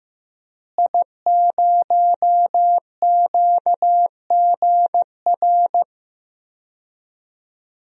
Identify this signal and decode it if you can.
Morse code, heard as a keyed tone, about 15 wpm, 700 Hz. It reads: I0QGR
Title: DXCC-100 Call Signs - 5 Characters - Repeated 2x - 15wpm